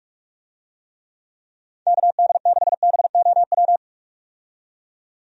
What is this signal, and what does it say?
Morse code, heard as a keyed tone, about 46 wpm, 700 Hz. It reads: KD6BOW
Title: USA Call Signs - 6 Characters - 45wpm